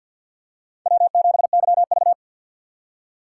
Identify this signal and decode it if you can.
Morse code, heard as a keyed tone, about 52 wpm, 700 Hz. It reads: W7YV
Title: USA Call Signs - 4 Characters - 2x Character Spacing - 50wpm